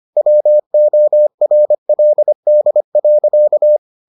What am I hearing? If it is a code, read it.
Morse code, heard as a keyed tone, about 25 wpm, 600 Hz. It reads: WORLD.